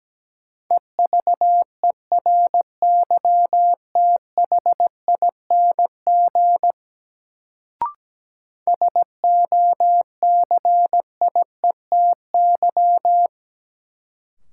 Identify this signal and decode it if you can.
Morse code, heard as a keyed tone, about 17 wpm, 700 Hz. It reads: EVERYTHING SOCIETY
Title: Sentences from Top 700 Words - New Word Drill - 17wpm